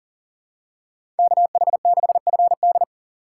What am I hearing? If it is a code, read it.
Morse code, heard as a keyed tone, about 40 wpm, 700 Hz. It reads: KH6FD
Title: USA Call Signs - 5 Characters - 3x Character Spacing - 40wpm